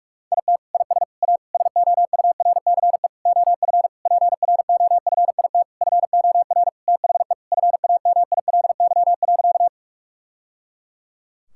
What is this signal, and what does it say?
Morse code, heard as a keyed tone, about 46 wpm, 700 Hz. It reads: IT IS A SOURCE OF PROFIT FOR THE FAMILY.
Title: Sentences from Top 1000 Words - Review Part 5 - 45wpm